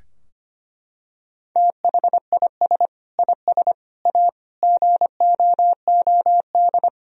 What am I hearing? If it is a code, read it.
Morse code, heard as a keyed tone, about 25 wpm, 700 Hz. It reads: THIS IS A GOOD
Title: Sentences from Top 1000 Words - Review Part 3 - 25wpm